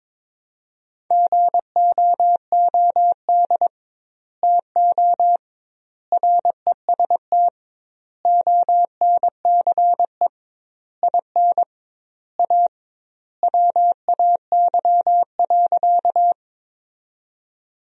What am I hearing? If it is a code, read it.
Morse code, heard as a keyed tone, about 22 wpm, 700 Hz. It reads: GOOD TO REST ONCE IN A WAY.
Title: 2x Word-Spacing - Sentences from Top 500 Words - 22wpm